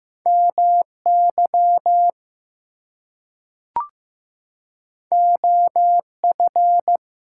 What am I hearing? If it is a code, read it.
Morse code, heard as a keyed tone, about 15 wpm, 700 Hz. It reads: MY OF
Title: Two Letter Words - 15wpm